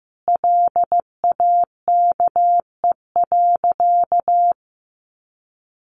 Morse code, heard as a keyed tone, about 15 wpm, 700 Hz. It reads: LAKE.
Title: Sentences from Top 1300 Words 15wpm